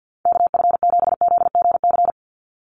Morse code, heard as a keyed tone, about 50 wpm, 700 Hz. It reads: KF7ZGC
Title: USA Call Signs 50wpm